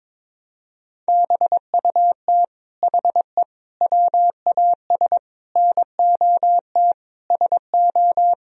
Morse code, heard as a keyed tone, about 22 wpm, 700 Hz. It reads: BUT HE WAS NOT SO
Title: Sentences from Top 1000 Words - Review Part 3 - 22wpm